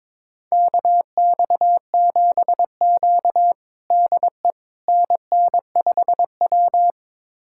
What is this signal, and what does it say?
Morse code, heard as a keyed tone, about 22 wpm, 700 Hz. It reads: KX7Q DE NN5W
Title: USA Callsign DE Callsign - 4 Characters - 22wpm